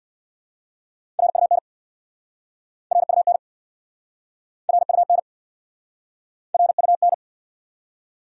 Morse code, heard as a keyed tone, about 60 wpm, 690 Hz. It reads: RUN RUN RUN RUN
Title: Speed-Racing Top 500 Words 50wpm